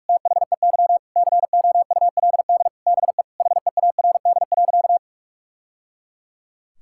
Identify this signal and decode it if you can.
Morse code, heard as a keyed tone, about 46 wpm, 700 Hz. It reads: THEY COULD BE HEARD.
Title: Sentences from Top 1000 Words - Review Part 5 - 45wpm